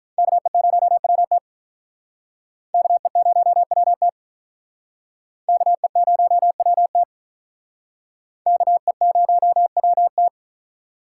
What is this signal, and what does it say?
Morse code, heard as a keyed tone, about 44 wpm, 700 Hz. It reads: KE0WT KE0WT KE0WT KE0WT
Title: Speed Racing - USA 2x2 Call Signs 35wpm